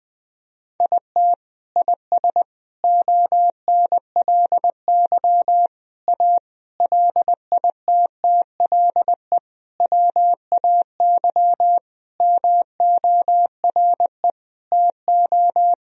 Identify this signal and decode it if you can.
Morse code, heard as a keyed tone, about 21 wpm, 700 Hz. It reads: IT IS ONLY A LITTLE WAY MORE TO
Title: Sentences from Top 1000 Words - Review Part 2 - 20wpm